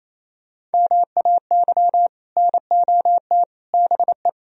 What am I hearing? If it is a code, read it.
Morse code, heard as a keyed tone, about 34 wpm, 700 Hz. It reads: MAY NOT BE
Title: Speed-Racing - Sets of 3 Words 28wpm